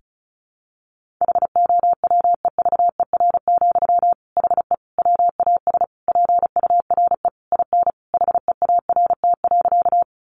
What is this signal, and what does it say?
Morse code, heard as a keyed tone, about 34 wpm, 700 Hz. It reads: HOWEVER, HE WAS PURE IN HEART.
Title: Sentences from Top 1500 Words 35wpm